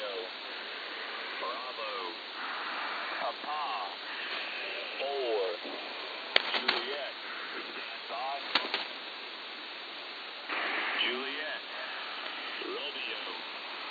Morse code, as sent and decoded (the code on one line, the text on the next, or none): none